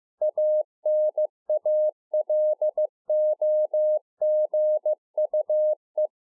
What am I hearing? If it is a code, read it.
Morse code, heard as a keyed tone, about 15 wpm, 610 Hz. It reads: ANALOGUE